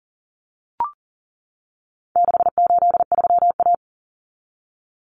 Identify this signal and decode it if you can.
Morse code, heard as a keyed tone, about 40 wpm, 700 Hz. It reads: 683A
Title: Common Cut Numbers - 4-Digit 40wpm